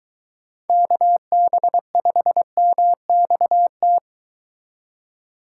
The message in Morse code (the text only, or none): KB5MXT